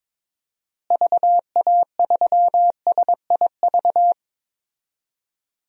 Morse code, heard as a keyed tone, about 21 wpm, 700 Hz. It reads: VA3SIV
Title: DXCC-100 Call Signs - 6 Characters - Repeated 2x - 22wpm